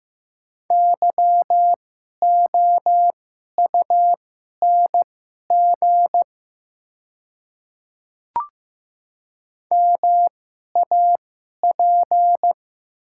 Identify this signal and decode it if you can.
Morse code, heard as a keyed tone, about 15 wpm, spaced as for 11 wpm, 700 Hz. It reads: YOUNG MAP
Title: 2x Character Spacing - Top 500 Words - 15wpm